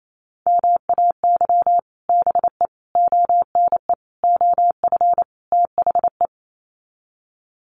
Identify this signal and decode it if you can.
Morse code, heard as a keyed tone, about 28 wpm, 700 Hz. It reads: MAY BE ONE OF THE
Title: Sets of 5 Words Encore2 28wpm